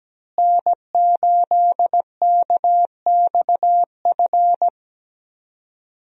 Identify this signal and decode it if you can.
Morse code, heard as a keyed tone, about 17 wpm, 700 Hz. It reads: N8KXF
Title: USA Call Signs - 5 Characters - Repeated 3x - 17wpm